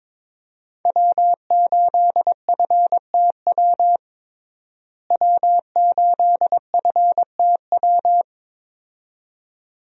Morse code, heard as a keyed tone, about 22 wpm, 700 Hz. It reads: W8FTW W8FTW
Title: USA Call Signs - 5 Characters - Repeated 2x - 22wpm